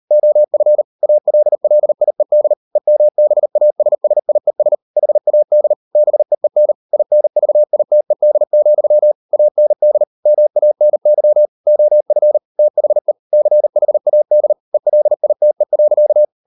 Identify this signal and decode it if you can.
Morse code, heard as a keyed tone, about 38 wpm, 600 Hz. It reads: OF ALLIED EMBASSIES HAD BEEN INVITED, AND MANY OF THE CHAD ELITE.